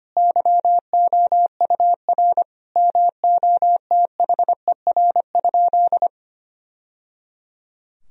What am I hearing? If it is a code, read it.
Morse code, heard as a keyed tone, about 25 wpm, 700 Hz. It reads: YOUR MOTHER?
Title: Sentences from Top 1000 Words - Review Part 4 - 25wpm